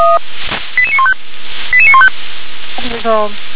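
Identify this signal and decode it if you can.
Five-tone selective call (ZVEI), heard as quick runs of five tones, tones of about 70 ms. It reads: 88915 88915
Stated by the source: ZVEI five-tone selective call received with an SDR.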